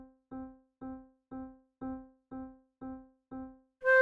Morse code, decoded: none